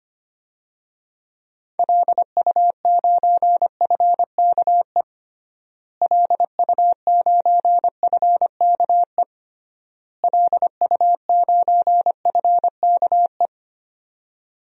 Morse code, heard as a keyed tone, about 25 wpm, 700 Hz. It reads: LU9FKE LU9FKE LU9FKE
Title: DXCC-100 Call Signs - 6 Characters - Repeated 3x - 25wpm